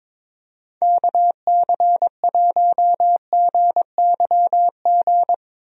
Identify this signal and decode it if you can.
Morse code, heard as a keyed tone, about 22 wpm, 700 Hz. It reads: KC1GYG
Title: USA Call Signs - 6 Characters - Repeated 2x - 22wpm